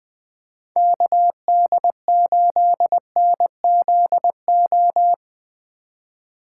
Morse code, heard as a keyed tone, about 20 wpm, 700 Hz. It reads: KD8NZO
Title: USA Call Signs - 6 Characters - 4x Character Spacing - 20wpm